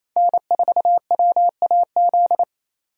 Morse code, heard as a keyed tone, about 28 wpm, 700 Hz. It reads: N4WAZ
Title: USA Call Signs - 5 Characters - 28wpm